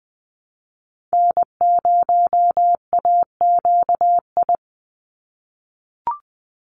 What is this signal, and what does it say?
Morse code, heard as a keyed tone, about 19 wpm, 700 Hz. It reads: N0AQI
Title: USA Call Signs 20wpm